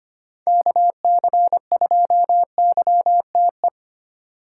Morse code, heard as a keyed tone, about 25 wpm, 700 Hz. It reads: KC2YTE
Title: USA Call Signs - 3x Character Spacing - 25wpm